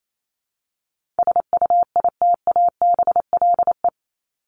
Morse code, heard as a keyed tone, about 28 wpm, 700 Hz. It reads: SUITABLE
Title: Top 1251-1500 Words 28wpm